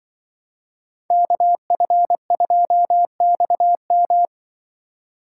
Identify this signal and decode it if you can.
Morse code, heard as a keyed tone, about 24 wpm, 700 Hz. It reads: KF2XM
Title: Speed Racing - USA 2x2 Call Signs 20wpm